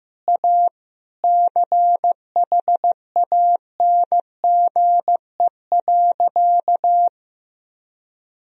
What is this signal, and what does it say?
Morse code, heard as a keyed tone, about 15 wpm, 700 Hz. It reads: A CHANGE.
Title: Sentences from Top 1000 Words - Review Part 1 - 15wpm